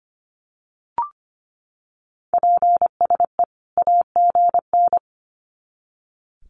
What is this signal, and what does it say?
Morse code, heard as a keyed tone, about 25 wpm, 700 Hz. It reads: PSE AGN
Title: Common QSO Phrases 25wpm